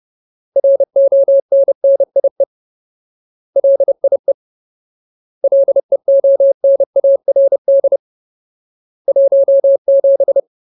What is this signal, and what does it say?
Morse code, heard as a keyed tone, about 30 wpm, 550 Hz. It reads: RONNIE LIE LEONARD 17
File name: Ultimate Word List / 30WPM Ultimate